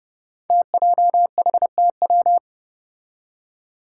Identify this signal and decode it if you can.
Morse code, heard as a keyed tone, about 28 wpm, 700 Hz. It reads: TJHTW
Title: Five Challenging Letter-Number Combinations 30wpm